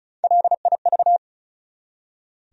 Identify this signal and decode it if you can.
Morse code, heard as a keyed tone, about 36 wpm, 700 Hz. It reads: LIV
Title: USA Common Names - Max 5 Letters - 35wpm